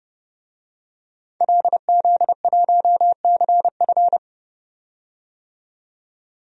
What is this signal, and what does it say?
Morse code, heard as a keyed tone, about 30 wpm, 700 Hz. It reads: LZ1CF
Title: DXCC-100 Call Signs - 5 Characters - 30wpm